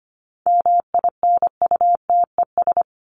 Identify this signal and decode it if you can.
Morse code, heard as a keyed tone, about 25 wpm, 700 Hz. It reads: MINUTES
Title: ES:S Suffix - Top 100 Words with Base Words 25wpm